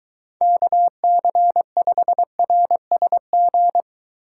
Morse code, heard as a keyed tone, about 22 wpm, 700 Hz. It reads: KC5RSG